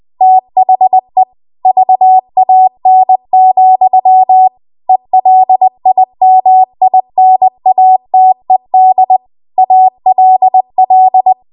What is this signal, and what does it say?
Morse code, heard as a keyed tone, about 20 wpm, 750 Hz. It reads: THE VAN, ELIMINATED ALL
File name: Morse / 20WPM